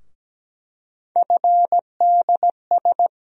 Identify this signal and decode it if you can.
Morse code, heard as a keyed tone, about 17 wpm, 700 Hz. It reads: FDS
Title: USA Call Sign Postfixes - 3 Letters 17wpm